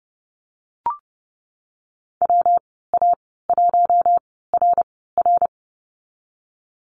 Morse code, heard as a keyed tone, about 30 wpm, spaced as for 17 wpm, 700 Hz. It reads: WA1RR
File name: Lesson 19g - Letter w - Call Signs 30wpm